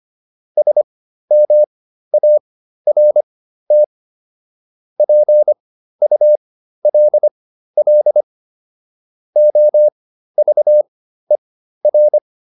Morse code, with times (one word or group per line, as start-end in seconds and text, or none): 0.6-3.8 SMART
5.0-8.2 PULL
9.4-12.2 OVER